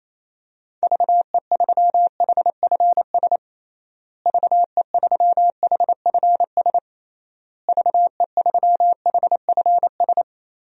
Morse code, heard as a keyed tone, about 27 wpm, 700 Hz. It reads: VE3HFS VE3HFS VE3HFS
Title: DXCC-100 Call Signs - 6 Characters - Repeated 3x - 28wpm